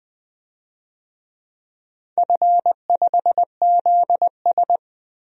Morse code, heard as a keyed tone, about 20 wpm, 700 Hz. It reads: F5ZS
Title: DXCC-100 Call Signs - 4 Characters - Repeated 2x - 20wpm